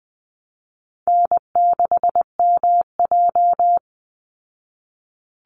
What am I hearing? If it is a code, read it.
Morse code, heard as a keyed tone, about 20 wpm, 700 Hz. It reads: N6MJ
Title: Easy Call Signs Repeated 3x - 20wpm